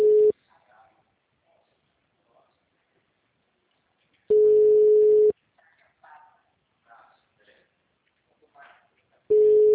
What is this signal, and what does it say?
Ringback tone (Europe, 425 Hz), with speech.
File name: out- -2010- - - .23486